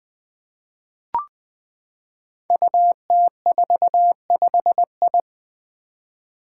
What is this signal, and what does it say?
Morse code, heard as a keyed tone, about 20 wpm, 700 Hz. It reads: UT45I
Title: Five Challenging Letter-Number Combinations 20wpm